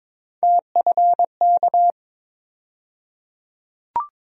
Morse code, heard as a keyed tone, about 22 wpm, 700 Hz. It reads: TFK
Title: USA Call Sign Postfixes - 3 Letters 22wpm